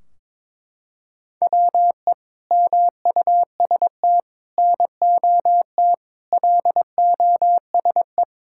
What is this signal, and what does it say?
Morse code, heard as a keyed tone, about 22 wpm, 700 Hz. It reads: WE MUST NOT LOSE
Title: Sentences from Top 1600 Words 22wpm